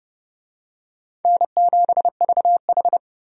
Morse code, heard as a keyed tone, about 29 wpm, 700 Hz. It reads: N7VH
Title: USA Call Signs - 4 Characters - 30wpm